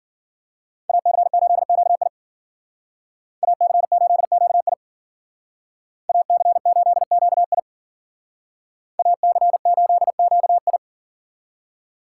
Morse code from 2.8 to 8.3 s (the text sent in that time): AC8QI AC8QI